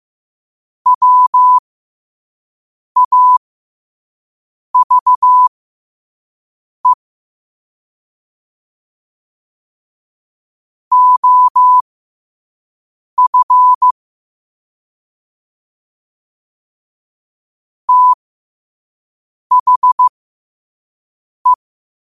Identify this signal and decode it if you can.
Morse code, heard as a keyed tone, about 15 wpm, spaced as for 5 wpm, 1000 Hz. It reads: WAVE OF THE
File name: qotdfiles / 5wpm